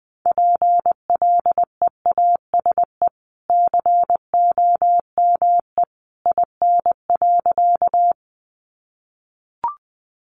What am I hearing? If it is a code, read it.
Morse code, heard as a keyed tone, about 20 wpm, 700 Hz. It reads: PLEASE COME IN.
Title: Sentences from Top 1500 Words 20wpm